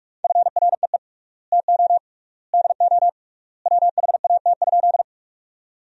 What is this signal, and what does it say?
Morse code, heard as a keyed tone, about 46 wpm, 700 Hz. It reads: FREE TO DO WHAT?
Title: 3x Word-Spacing - Sentences from Top 500 Words - 45wpm